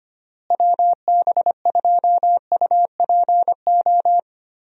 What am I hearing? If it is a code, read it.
Morse code, heard as a keyed tone, about 25 wpm, 700 Hz. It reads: WB2UPO
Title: USA Call Signs - 6 Characters - Repeated 3x - 25wpm